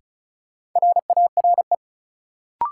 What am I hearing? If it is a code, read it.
Morse code, heard as a keyed tone, about 34 wpm, 700 Hz. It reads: RARE
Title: Sentences from Top 1100 Words - New Word Drill - 35wpm